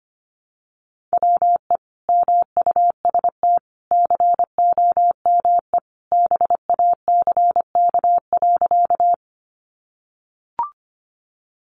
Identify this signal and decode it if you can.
Morse code, heard as a keyed tone, about 25 wpm, 700 Hz. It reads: WE MUST COME BACK.
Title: Sentences from Top 200 Words 25wpm